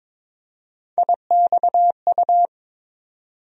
Morse code, heard as a keyed tone, about 22 wpm, 700 Hz. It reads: IXU
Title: USA Call Sign Postfixes - 3 Letters 22wpm